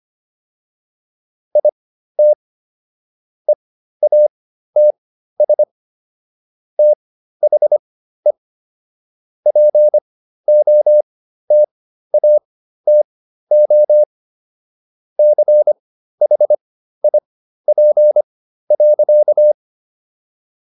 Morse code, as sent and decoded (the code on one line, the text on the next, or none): .. - / . .- - ... / - .... / .--. --- - .- - --- / -.-. .... .. .--. .-.-.-
IT EATS THE POTATO CHIP.